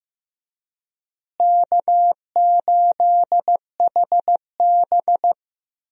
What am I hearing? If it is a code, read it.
Morse code, heard as a keyed tone, about 15 wpm, 700 Hz. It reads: K8HB
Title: USA Call Signs - 4 Characters - Repeated 2x - 15wpm